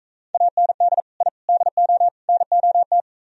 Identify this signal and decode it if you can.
Morse code, heard as a keyed tone, about 43 wpm, 700 Hz. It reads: AND I DO NOT